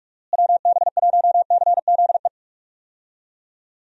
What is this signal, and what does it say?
Morse code, heard as a keyed tone, about 47 wpm, 700 Hz. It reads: WB1CZE
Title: USA Call Signs - 6 Characters - Repeated 3x - 45wpm